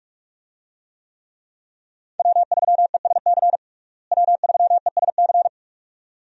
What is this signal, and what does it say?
Morse code, heard as a keyed tone, about 43 wpm, 700 Hz. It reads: W3ESC W3ESC
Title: USA Call Signs - 5 Characters - Repeated 2x - 45wpm